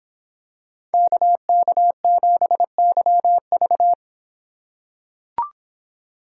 Morse code, heard as a keyed tone, about 26 wpm, 700 Hz. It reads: KK7YV